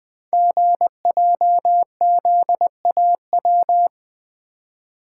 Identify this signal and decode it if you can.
Morse code, heard as a keyed tone, about 20 wpm, 700 Hz. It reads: GJZAW
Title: Code Groups - 5 Letters - 20wpm